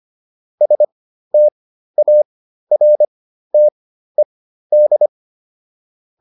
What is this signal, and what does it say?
Morse code, heard as a keyed tone, about 25 wpm, spaced as for 13 wpm, 600 Hz. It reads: STARTED